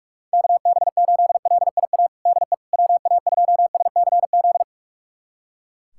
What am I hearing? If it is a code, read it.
Morse code, heard as a keyed tone, about 41 wpm, 700 Hz. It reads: KB8LIA DE WA2SCZ
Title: USA Callsign DE Callsign - 6 Characters - 45wpm